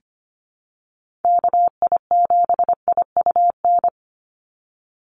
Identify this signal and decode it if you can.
Morse code, heard as a keyed tone, about 25 wpm, 700 Hz. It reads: KI7IUN
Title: USA Call Signs 25wpm